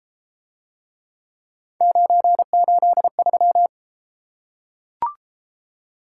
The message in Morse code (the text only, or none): 983